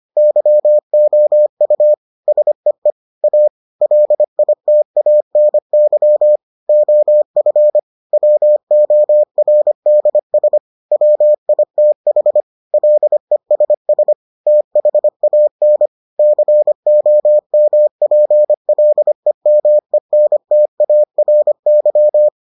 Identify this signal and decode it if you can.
Morse code, heard as a keyed tone, about 25 wpm, 600 Hz. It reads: YOU SEE A LITANY OF WORDS WITH LESS THAN COMPLEMENTARY